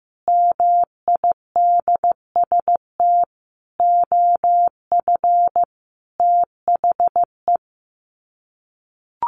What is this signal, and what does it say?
Morse code, heard as a keyed tone, about 15 wpm, 700 Hz. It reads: MIDST OF THE
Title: Sets of 5 Words 15wpm